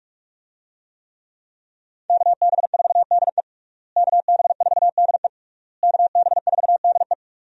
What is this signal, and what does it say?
Morse code, heard as a keyed tone, about 46 wpm, 700 Hz. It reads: KB4DE KB4DE KB4DE
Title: USA Call Signs - 5 Characters - Repeated 3x - 45wpm